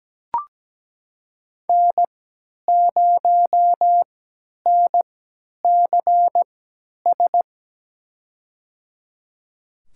Morse code, heard as a keyed tone, about 17 wpm, spaced as for 10 wpm, 700 Hz. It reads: N0NCS
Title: USA Call Signs - 5 Characters - 3x Character Spacing - 17wpm